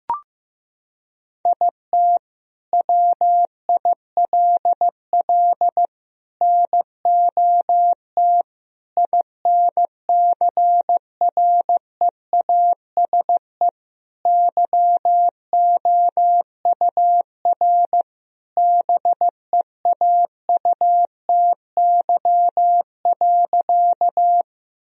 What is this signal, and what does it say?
Morse code, heard as a keyed tone, about 15 wpm, 700 Hz. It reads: IT WILL NOT INCREASE YOUR BEAUTY.
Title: Sentences from Top 1000 Words - Review Part 4 - 15wpm